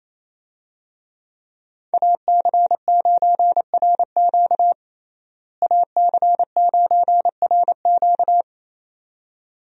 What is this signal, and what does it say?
Morse code, heard as a keyed tone, about 27 wpm, 700 Hz. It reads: AC9RQ AC9RQ
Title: USA Call Signs - 5 Characters - Repeated 2x - 28wpm